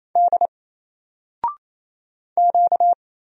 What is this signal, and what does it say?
Morse code, heard as a keyed tone, about 28 wpm, 700 Hz. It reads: D Q
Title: Single Character (Letter:Number:Punctuation:Prosign) Alt 28wpm